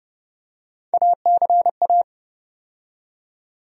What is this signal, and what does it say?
Morse code, heard as a keyed tone, about 30 wpm, 700 Hz. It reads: ACA
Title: USA Call Sign Postfixes - 3 Letters 30wpm